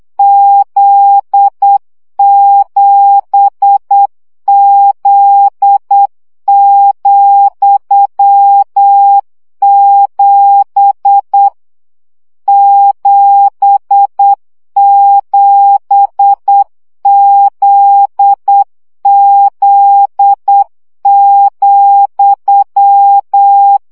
Morse code, heard as a keyed tone, about 8 wpm, 800 Hz. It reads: Z7Z,7 77ZZ,
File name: CW gyakorlas z 7 vesszo 7 baud